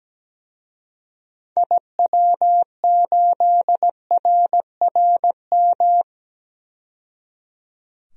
Morse code, heard as a keyed tone, about 17 wpm, 700 Hz. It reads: IW8RRM
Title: DXCC-100 Call Signs - 6 Characters - 17wpm